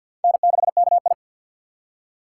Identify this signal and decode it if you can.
Morse code, heard as a keyed tone, about 50 wpm, 700 Hz. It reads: N6KI
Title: USA Call Signs - 3x Character Spacing - 50wpm